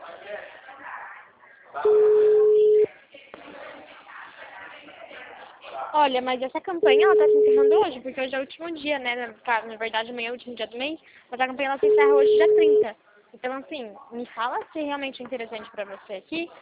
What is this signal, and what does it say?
Ringback tone (Europe, 425 Hz), with speech.